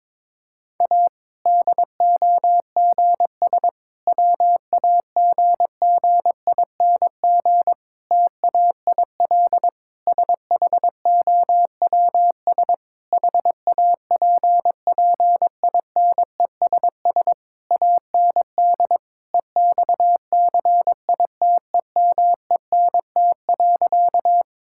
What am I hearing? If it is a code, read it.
Morse code, heard as a keyed tone, about 22 wpm, 700 Hz. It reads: A DOGS WAGGING TAIL SHOWS HAPPINESS AND EXCITEMENT.
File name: Difficult Sentences with Hints - 22wpm